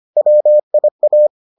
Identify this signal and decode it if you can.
Morse code, heard as a keyed tone, about 25 wpm, 600 Hz. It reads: WIA